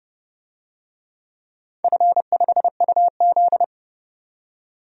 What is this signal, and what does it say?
Morse code, heard as a keyed tone, about 28 wpm, 700 Hz. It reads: F5UZ
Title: DXCC-100 Call Signs - 4 Characters - 30wpm